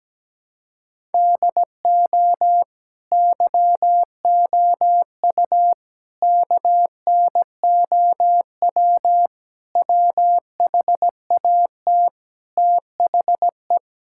Morse code, heard as a keyed tone, about 17 wpm, 700 Hz. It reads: DO YOU KNOW WHAT THE
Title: Sentences from Top 1000 Words - Review Part 6 - 17wpm